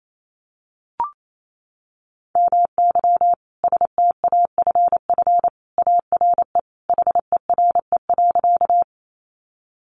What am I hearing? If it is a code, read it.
Morse code, heard as a keyed tone, about 28 wpm, 700 Hz. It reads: MY STAFF ARE HERE.
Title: Sentences from Top 1200 Words 28wpm